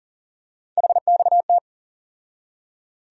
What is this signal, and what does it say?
Morse code, heard as a keyed tone, about 40 wpm, 700 Hz. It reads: HXT
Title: USA Call Sign Postfixes - 3 Letters 40wpm